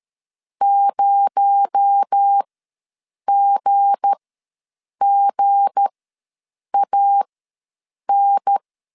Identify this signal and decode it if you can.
Morse code, heard as a keyed tone, about 13 wpm, spaced as for 7 wpm, 800 Hz. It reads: 0GGAN
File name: LEZIONI / 7 GF9Z / 7 G ws 12wpm 3140